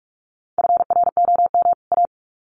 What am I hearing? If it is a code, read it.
Morse code, heard as a keyed tone, about 44 wpm, 700 Hz. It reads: FROM A
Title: Sets of 2 Words 45wpm